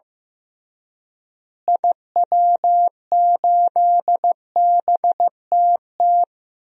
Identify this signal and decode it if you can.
Morse code, heard as a keyed tone, about 15 wpm, 700 Hz. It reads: IW8BTT
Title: DXCC-100 Call Signs - 6 Characters - Repeated 3x - 15wpm